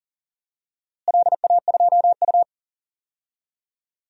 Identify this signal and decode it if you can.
Morse code, heard as a keyed tone, about 37 wpm, 690 Hz. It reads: LA2U
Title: DXCC-100 Call Signs - 4 Characters - Repeated 2x - 40wpm